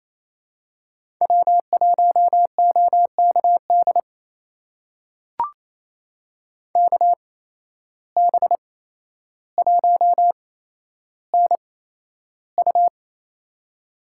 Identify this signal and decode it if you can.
Morse code, heard as a keyed tone, about 29 wpm, spaced as for 12 wpm, 700 Hz. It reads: W1OKD KB1NU
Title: USA Call Signs - 5 Characters - 8x Character Spacing - 28wpm